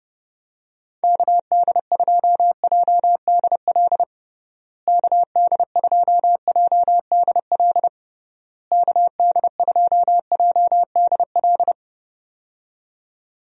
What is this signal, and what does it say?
Morse code, heard as a keyed tone, about 30 wpm, 700 Hz. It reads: KD2JDL KD2JDL KD2JDL
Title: USA Call Signs - 6 Characters - Repeated 3x - 30wpm